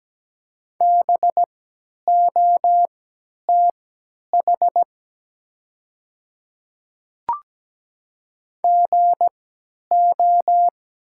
Morse code, heard as a keyed tone, about 17 wpm, spaced as for 10 wpm, 700 Hz. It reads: BOTH GO
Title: 3x Character Spacing - Top 300 Words - 17wpm